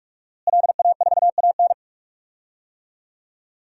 Morse code, heard as a keyed tone, about 43 wpm, 700 Hz. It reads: LA4AN
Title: DXCC-100 Call Signs - 5 Characters - Repeated 2x - 45wpm